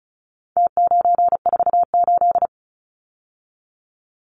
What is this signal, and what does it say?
Morse code, heard as a keyed tone, about 34 wpm, 700 Hz. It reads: T948
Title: Numbers - Contest Style - 4-Digit 35wpm